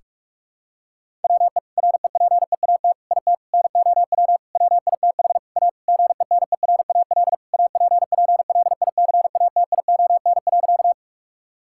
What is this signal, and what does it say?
Morse code, heard as a keyed tone, about 46 wpm, 700 Hz. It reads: WE REPEAT IT NOW WITH A GENERAL APPLICATION.
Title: Sentences from Top 1600 Words 45wpm